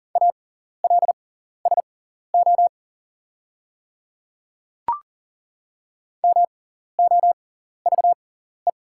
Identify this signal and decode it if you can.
Morse code, heard as a keyed tone, about 40 wpm, spaced as for 14 wpm, 690 Hz. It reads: ALSO MOVE
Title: 6x Character Spacing - Top 200 Words - 40wpm